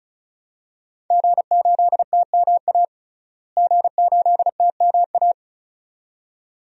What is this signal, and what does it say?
Morse code, heard as a keyed tone, about 34 wpm, 700 Hz. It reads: G8TMA G8TMA
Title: DXCC-100 Call Signs - 5 Characters - Repeated 2x - 35wpm